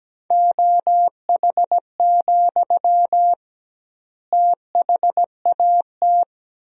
Morse code, heard as a keyed tone, about 17 wpm, 700 Hz. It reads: OH, THAT
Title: 2x Word-Spacing - Sentences from Top 500 Words - 17wpm